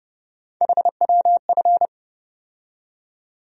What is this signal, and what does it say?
Morse code, heard as a keyed tone, about 30 wpm, 700 Hz. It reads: HWF
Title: USA Call Sign Postfixes - 3 Letters 30wpm